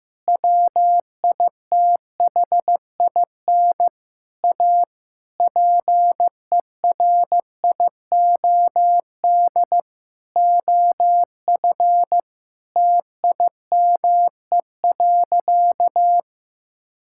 Morse code, heard as a keyed tone, about 15 wpm, 700 Hz. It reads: WITHIN A PERIOD OF TIME.
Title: Sentences from Top 1000 Words - Review Part 5 - 15wpm